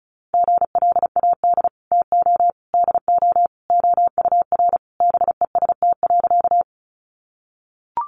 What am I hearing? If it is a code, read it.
Morse code, heard as a keyed tone, about 34 wpm, 700 Hz. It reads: GLAD TO DO OUR BEST.
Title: Sentences from Top 800 Words 35wpm